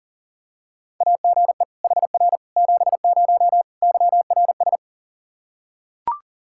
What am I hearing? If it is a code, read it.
Morse code, heard as a keyed tone, about 40 wpm, 700 Hz. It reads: AGE HR 70 YRS
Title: Practice QSO - Exchange 3 - More Info - 40wpm